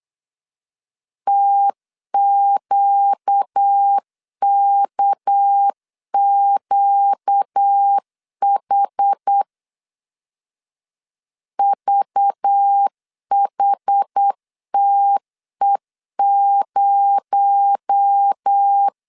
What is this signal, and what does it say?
Morse code, heard as a keyed tone, about 8 wpm, 800 Hz. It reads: TQKQH VHTE0
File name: CW / DET / 1 6 1500C S A 8wpm 5203